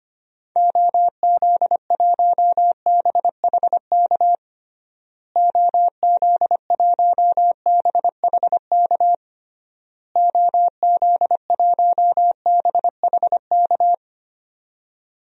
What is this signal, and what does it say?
Morse code, heard as a keyed tone, about 25 wpm, 700 Hz. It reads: OZ1BHK OZ1BHK OZ1BHK
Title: DXCC-100 Call Signs - 6 Characters - Repeated 3x - 25wpm